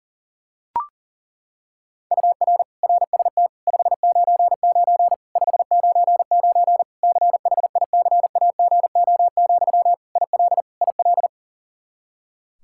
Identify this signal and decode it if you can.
Morse code, heard as a keyed tone, about 40 wpm, 700 Hz. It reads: UR RST 599 599 CHICAGO, IL IL
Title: Practice QSO - Exchange 1 - Beginning - 40wpm